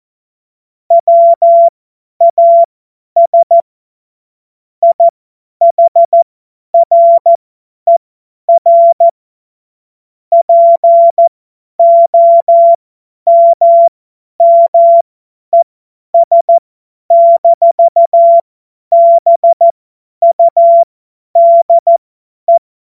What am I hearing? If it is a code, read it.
Morse code, heard as a keyed tone, about 14 wpm, spaced as for 10 wpm, 680 Hz. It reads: WAS IHRER POMMES-BUDE
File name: Morsekurs / Witze / Witz 13. Januar 2026 50 F70